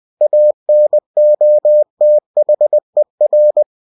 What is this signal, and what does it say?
Morse code, heard as a keyed tone, about 20 wpm, 600 Hz. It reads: ANOTHER